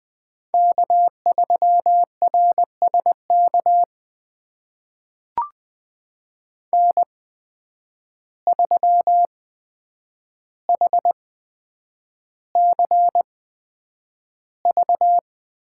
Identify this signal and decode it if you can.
Morse code, heard as a keyed tone, about 20 wpm, spaced as for 9 wpm, 700 Hz. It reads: K3RSK N3HCV
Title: USA Call Signs - 5 Characters - 8x Character Spacing - 20wpm